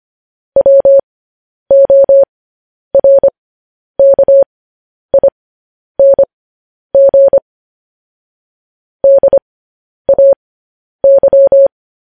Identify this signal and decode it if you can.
Morse code, heard as a keyed tone, about 25 wpm, spaced as for 10 wpm, 550 Hz. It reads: WORKING DAY
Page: news-headlines-in-morse-code-at-10-wpm